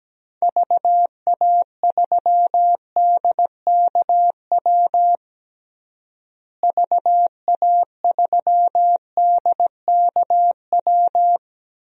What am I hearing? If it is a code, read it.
Morse code, heard as a keyed tone, about 17 wpm, 700 Hz. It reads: VA3DKW VA3DKW
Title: DXCC-100 Call Signs - 6 Characters - Repeated 3x - 17wpm